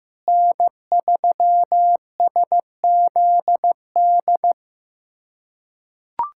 Morse code, heard as a keyed tone, about 15 wpm, 700 Hz. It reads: N3SZD
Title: USA Callsign DE Callsign 15wpm